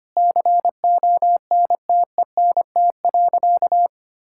Morse code, heard as a keyed tone, about 25 wpm, 700 Hz. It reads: CONTENT.
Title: Sentences from Top 1000 Words - Review Part 6 - 25wpm